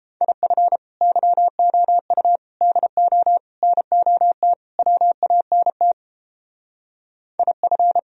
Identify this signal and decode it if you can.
Morse code, heard as a keyed tone, about 31 wpm, 700 Hz. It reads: IF YOU DO NOT WANT IF